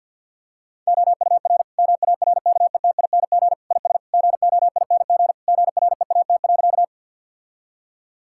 Morse code, heard as a keyed tone, about 50 wpm, 700 Hz. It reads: OUR MARKETING IS GOING GREAT.
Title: Sentences from Top 1000 Words - Review Part 5 - 50wpm